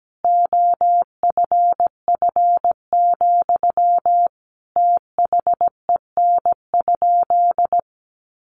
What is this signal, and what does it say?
Morse code, heard as a keyed tone, about 17 wpm, 700 Hz. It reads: OFF, THEN?
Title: Sentences from Top 1300 Words 17wpm